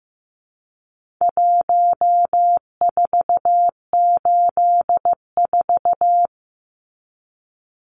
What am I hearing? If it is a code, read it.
Morse code, heard as a keyed tone, about 15 wpm, 700 Hz. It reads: 1484
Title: Numbers - Contest Style - 4-Digit 15wpm